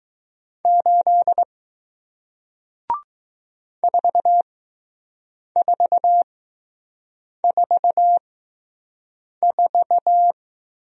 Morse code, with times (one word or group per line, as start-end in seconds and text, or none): 0.7-1.4 8
3.8-10.3 4444